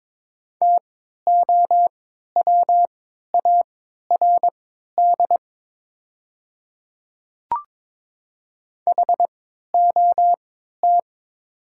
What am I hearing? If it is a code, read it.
Morse code, heard as a keyed tone, about 23 wpm, spaced as for 13 wpm, 700 Hz. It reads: TOWARD HOT
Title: 3x Character Spacing - Top 500 Words - 22wpm